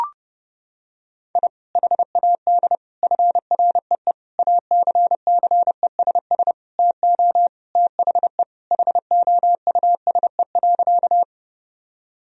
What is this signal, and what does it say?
Morse code, heard as a keyed tone, about 30 wpm, 700 Hz. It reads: I HAD FREE ACCESS TO THE HOUSE.
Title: Sentences from Top 1000 Words - Review Part 4 - 30wpm